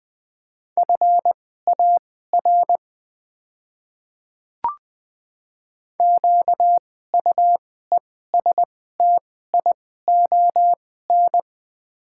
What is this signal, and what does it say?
Morse code, heard as a keyed tone, about 20 wpm, spaced as for 15 wpm, 690 Hz. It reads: FAR QUESTION
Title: 2x Character Spacing - Top 500 Words - 20wpm